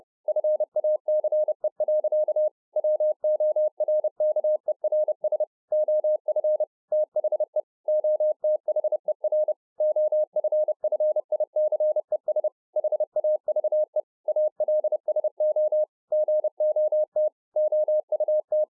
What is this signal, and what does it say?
Morse code, heard as a keyed tone, about 30 wpm, 610 Hz. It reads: FACE. WORKERS OF THE OTHER OFFICES HAVE ALSO GOT OUT